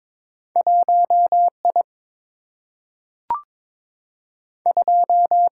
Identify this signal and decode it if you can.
Morse code, heard as a keyed tone, about 22 wpm, 700 Hz. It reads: 1I 2
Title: Single-Letter Single-Number 22wpm